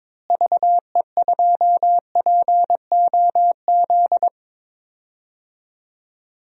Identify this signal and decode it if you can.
Morse code, heard as a keyed tone, about 21 wpm, 700 Hz. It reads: VE2POZ
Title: DXCC-100 Call Signs - 6 Characters - Repeated 2x - 22wpm